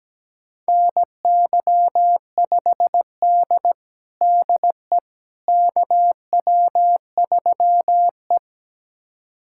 Morse code, heard as a keyed tone, about 17 wpm, 700 Hz. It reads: NY5D DE KW3E
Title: USA Callsign DE Callsign - 4 Characters - 17wpm